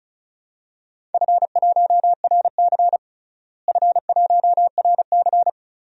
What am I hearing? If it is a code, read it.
Morse code, heard as a keyed tone, about 36 wpm, 700 Hz. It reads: F1RC F1RC
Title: DXCC-100 Call Signs - 4 Characters - Repeated 3x - 35wpm